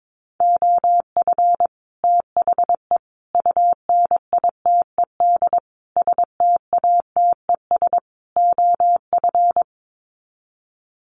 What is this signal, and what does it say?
Morse code, heard as a keyed tone, about 22 wpm, 700 Hz. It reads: OF THE UNITED STATES OF
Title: Sets of 5 Words Encore 22wpm